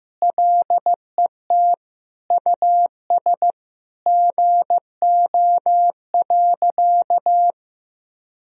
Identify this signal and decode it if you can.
Morse code, heard as a keyed tone, about 15 wpm, 700 Hz. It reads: LET US GO.
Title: Sentences from Top 1000 Words - Review Part 2 - 15wpm